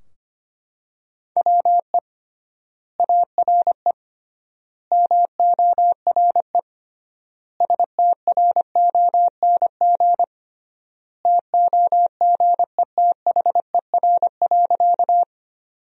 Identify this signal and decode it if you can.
Morse code, heard as a keyed tone, about 25 wpm, 700 Hz. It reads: WE ARE MORE STRONG TOGETHER.
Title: 3x Word-Spacing - Sentences from Top 500 Words - 25wpm